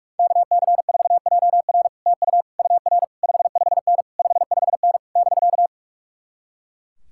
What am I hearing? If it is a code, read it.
Morse code, heard as a keyed tone, about 66 wpm, spaced as for 55 wpm, 700 Hz. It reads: KC4JR TU UR 55N 55N <BK>
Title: Speed-Racing Common QSO Phrases Encore - 45wpm